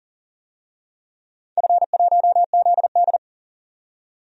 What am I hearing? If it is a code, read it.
Morse code, heard as a keyed tone, about 39 wpm, 700 Hz. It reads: F1ZD
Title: DXCC-100 Call Signs - 4 Characters - 40wpm